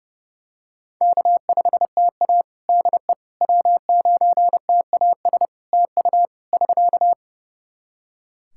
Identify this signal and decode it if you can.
Morse code, heard as a keyed tone, about 32 wpm, 700 Hz. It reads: K5TA DE W9TAS TU <SK>
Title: Practice QSO - Exchange 4 - Ending - 30wpm